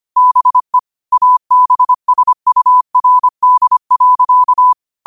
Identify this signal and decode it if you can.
Morse code, heard as a keyed tone, about 25 wpm, 1000 Hz. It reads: BE ABSURD.
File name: qotdfiles / 25wpm